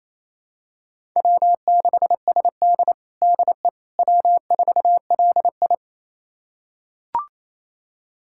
Morse code, heard as a keyed tone, about 28 wpm, 700 Hz. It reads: W6SD DE W4LI
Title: USA Callsign DE Callsign - 4 Characters - 28wpm